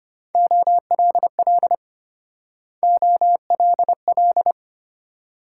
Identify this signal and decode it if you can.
Morse code, heard as a keyed tone, about 32 wpm, 700 Hz. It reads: OLL OLL